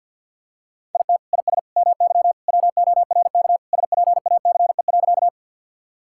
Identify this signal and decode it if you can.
Morse code, heard as a keyed tone, about 50 wpm, 700 Hz. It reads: IT IS MY WORK SPACE.
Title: Sentences from Top 1000 Words - Review Part 5 - 50wpm